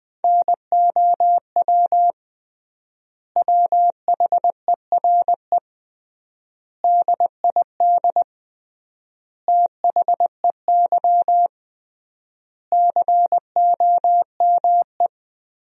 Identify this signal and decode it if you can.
Morse code, heard as a keyed tone, about 20 wpm, 700 Hz. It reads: NOW WHERE DID THEY COME
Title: 3x Word-Spacing - Sentences from Top 200 Words - 20wpm